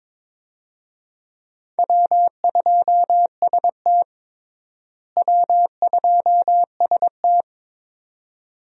Morse code, heard as a keyed tone, about 22 wpm, 700 Hz. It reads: W2ST W2ST
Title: USA Call Signs - 4 Characters - Repeated 2x - 22wpm